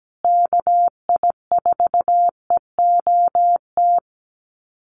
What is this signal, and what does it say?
Morse code, heard as a keyed tone, about 17 wpm, 700 Hz. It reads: KI4EOT
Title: USA Call Signs Repeated 3x 17wpm